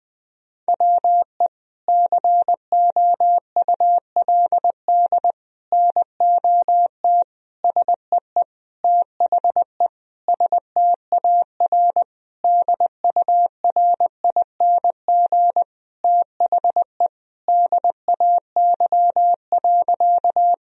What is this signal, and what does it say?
Morse code, heard as a keyed tone, about 21 wpm, 700 Hz. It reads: WE COULD NOT SEE THE STAR DURING THE DAY.
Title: Sentences from Top 1000 Words - Review Part 2 - 20wpm